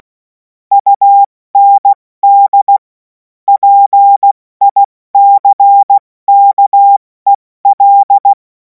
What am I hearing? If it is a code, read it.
Morse code, heard as a keyed tone, about 16 wpm, 800 Hz. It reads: UND PICKEL
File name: Morsekurs / Witze / Witz 17. Februar 2026 70 F80